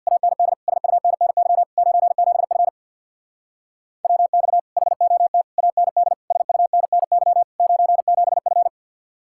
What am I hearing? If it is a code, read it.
Morse code, heard as a keyed tone, about 65 wpm, 700 Hz. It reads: AND SUNNY 97F WX HOT AND SUNNY 97F